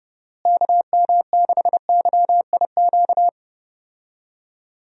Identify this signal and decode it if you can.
Morse code, heard as a keyed tone, about 30 wpm, 700 Hz. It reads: KM6YIQ
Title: USA Call Signs - 6 Characters - Repeated 2x - 30wpm